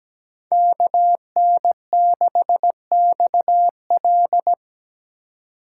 Morse code, heard as a keyed tone, about 17 wpm, 700 Hz. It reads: KN6XL
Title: USA Call Signs - 4x Character Spacing - 17wpm